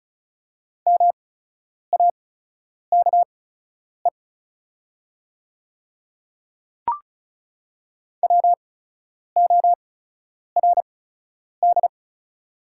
Morse code, heard as a keyed tone, about 36 wpm, spaced as for 10 wpm, 690 Hz. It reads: MAKE WORD
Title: 8x Character Spacing - Top 100 Words - 35wpm